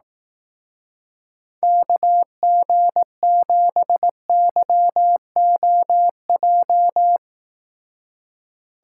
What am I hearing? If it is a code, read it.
Morse code, heard as a keyed tone, about 18 wpm, 700 Hz. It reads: KG7YOJ
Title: Speed Racing - USA 2x3 Call Signs 15wpm